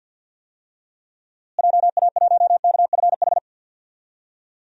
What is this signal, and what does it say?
Morse code, heard as a keyed tone, about 49 wpm, 700 Hz. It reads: JA1KFH
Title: DXCC-100 Call Signs - 6 Characters - Repeated 3x - 50wpm